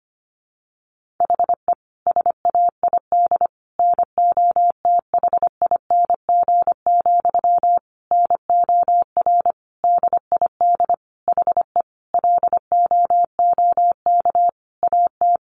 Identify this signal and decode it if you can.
Morse code, heard as a keyed tone, about 25 wpm, 700 Hz. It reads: HE SAID NOTHING, NOR DID HE LOOK AT
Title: Sentences from Top 1500 Words 25wpm